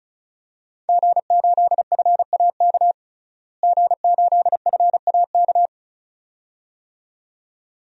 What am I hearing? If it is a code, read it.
Morse code, heard as a keyed tone, about 36 wpm, 700 Hz. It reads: G8FAK G8FAK
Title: DXCC-100 Call Signs - 5 Characters - Repeated 2x - 35wpm